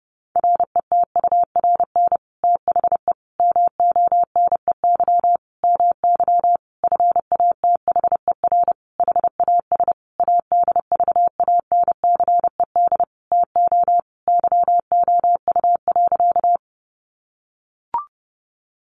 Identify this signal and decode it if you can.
Morse code, heard as a keyed tone, about 30 wpm, 700 Hz. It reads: RETURN THE MONEY MY FATHER HAS ADVANCED TO YOU.
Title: Sentences from Top 1200 Words 30wpm